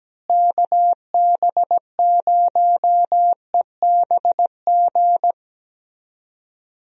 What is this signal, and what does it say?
Morse code, heard as a keyed tone, about 17 wpm, 700 Hz. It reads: KB0EBG
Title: USA Call Signs - 6 Characters - Repeated 2x - 17wpm